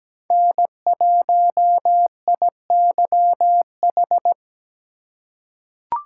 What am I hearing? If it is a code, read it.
Morse code, heard as a keyed tone, about 17 wpm, 700 Hz. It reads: N1IYH
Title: USA Call Signs - 3x Character Spacing - 17wpm